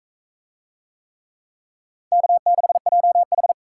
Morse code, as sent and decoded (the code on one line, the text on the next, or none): -.- -.... .--- ....
K6JH